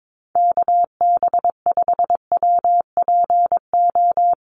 Morse code, heard as a keyed tone, about 22 wpm, 700 Hz. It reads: KB5WPO
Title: USA Call Signs Repeated 2x 22wpm